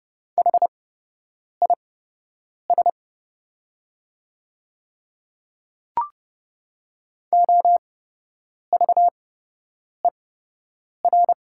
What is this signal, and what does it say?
Morse code, heard as a keyed tone, about 30 wpm, spaced as for 8 wpm, 700 Hz. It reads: HIS OVER
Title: 8x Character Spacing - Top 200 Words - 30wpm